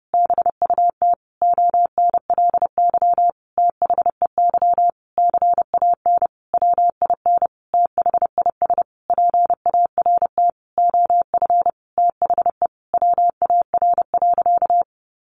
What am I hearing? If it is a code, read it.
Morse code, heard as a keyed tone, about 30 wpm, 700 Hz. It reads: BUT ONLY THEY CAN WIN THIS PART OF THE WAR.
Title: Sentences from Top 1500 Words 30wpm